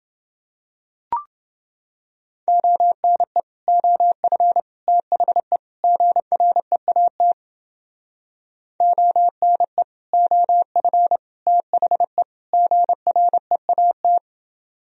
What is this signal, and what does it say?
Morse code, heard as a keyed tone, about 29 wpm, 700 Hz. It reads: ONE OF THE GREAT ONE OF THE GREAT